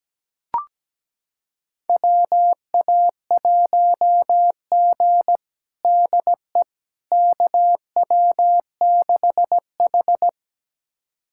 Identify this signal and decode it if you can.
Morse code, heard as a keyed tone, about 17 wpm, 700 Hz. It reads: WA1G DE KW6H
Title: USA Callsign DE Callsign - 4 Characters - 17wpm